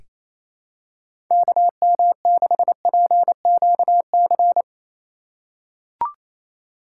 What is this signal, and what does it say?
Morse code, heard as a keyed tone, about 28 wpm, 700 Hz. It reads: KM6PQC
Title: USA Call Signs - 6 Characters - 3x Character Spacing - 28wpm